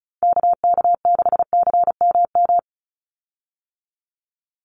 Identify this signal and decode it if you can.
Morse code, heard as a keyed tone, about 34 wpm, 700 Hz. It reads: KK6CMM
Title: USA Call Signs 35wpm